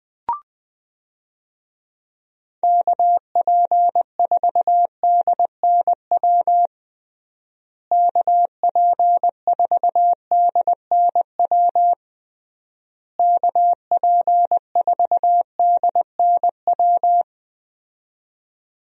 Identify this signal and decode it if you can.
Morse code, heard as a keyed tone, about 21 wpm, 700 Hz. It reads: KP4DNW KP4DNW KP4DNW
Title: USA Call Signs - 6 Characters - Repeated 3x - 20wpm